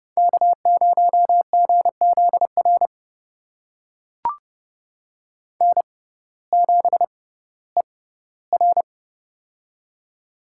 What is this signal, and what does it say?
Morse code, heard as a keyed tone, about 30 wpm, spaced as for 17 wpm, 700 Hz. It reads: K0GZR N7ER
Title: USA Call Signs - 6x Character Spacing - 30wpm